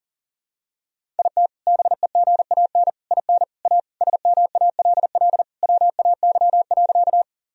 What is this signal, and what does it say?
Morse code, heard as a keyed tone, about 37 wpm, 700 Hz. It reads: IT BEGAN IN A SMALL WAY.
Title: Sentences from Top 1000 Words - Review Part 1 - 40wpm